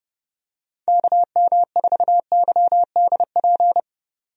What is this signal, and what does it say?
Morse code, heard as a keyed tone, about 30 wpm, 700 Hz. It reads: KM4YDP
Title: USA Call Signs - 6 Characters - 30wpm